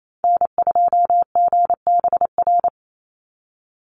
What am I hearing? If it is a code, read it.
Morse code, heard as a keyed tone, about 28 wpm, 700 Hz. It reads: N2GBR
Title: NA SOTA Activator Callsigns 28wpm